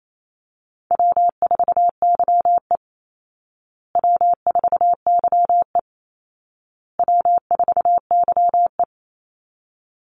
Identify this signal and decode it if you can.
Morse code, heard as a keyed tone, about 28 wpm, 700 Hz. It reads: W4YE W4YE W4YE
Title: Easy Call Signs Repeated 3x - 28wpm